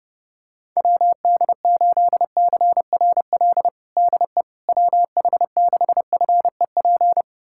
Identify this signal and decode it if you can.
Morse code, heard as a keyed tone, about 30 wpm, 700 Hz. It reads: WD8CRL DE WH6FEP
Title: USA Callsign DE Callsign - 6 Characters - 30wpm